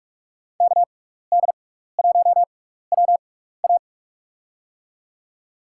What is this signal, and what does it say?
Morse code, heard as a keyed tone, about 46 wpm, 700 Hz. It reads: K D 1 W A
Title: USA Call Signs - 5 Characters - 6x Character Spacing - 45wpm